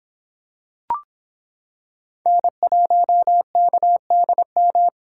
Morse code, heard as a keyed tone, about 26 wpm, 700 Hz. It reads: N1KDM